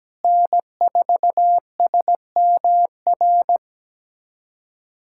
Morse code, heard as a keyed tone, about 17 wpm, 700 Hz. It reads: N4SMR
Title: USA Callsign DE Callsign - 5 Characters - 17wpm